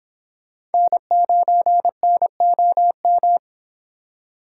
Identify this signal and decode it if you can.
Morse code, heard as a keyed tone, about 28 wpm, 700 Hz. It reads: N9NOM